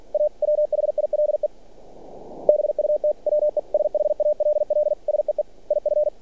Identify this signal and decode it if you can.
Morse code, heard as a keyed tone, about 46 wpm, 600 Hz. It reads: A WHILE BUT WE SHALL SEE IF
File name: archived / CRSnet